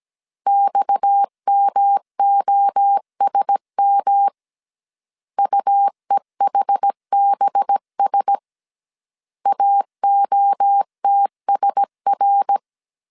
Morse code, heard as a keyed tone, about 17 wpm, 800 Hz. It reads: XMOSM UEHBS AOTSR
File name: CW / DET / 1 6 1500C S A 16wpm 2635